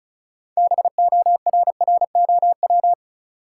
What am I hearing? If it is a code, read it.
Morse code, heard as a keyed tone, about 36 wpm, 700 Hz. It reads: BORROW
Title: Sentences from Top 1500 Words - New Word Drill - 35wpm